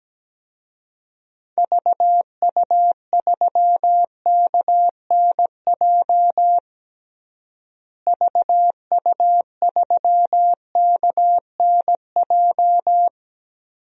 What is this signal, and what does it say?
Morse code, heard as a keyed tone, about 17 wpm, 700 Hz. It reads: VU3KNJ VU3KNJ
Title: DXCC-100 Call Signs - 6 Characters - Repeated 2x - 17wpm